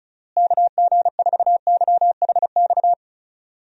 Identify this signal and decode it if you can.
Morse code, heard as a keyed tone, about 36 wpm, 700 Hz. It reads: KG4YHX
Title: USA Call Signs - 6 Characters - Repeated 3x - 35wpm